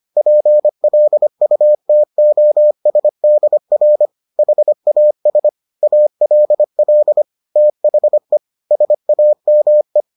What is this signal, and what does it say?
Morse code, heard as a keyed tone, about 25 wpm, 600 Hz. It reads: PLUTOSDR HAS ALL THE SAME